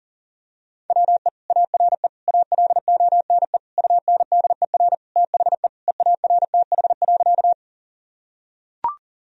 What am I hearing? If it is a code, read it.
Morse code, heard as a keyed tone, about 40 wpm, 700 Hz. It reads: WE ARE ALONE UNDER THE EARTH.
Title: Sentences from Top 1000 Words - Review Part 1 - 40wpm